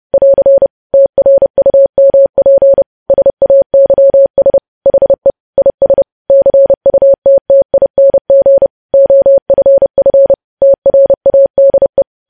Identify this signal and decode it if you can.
Morse code, heard as a keyed tone, about 30 wpm, 550 Hz. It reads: <AR> TRUMP SAYS HE IS CUTTING OFF TRADE
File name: podcast / News headlines-1x-30x30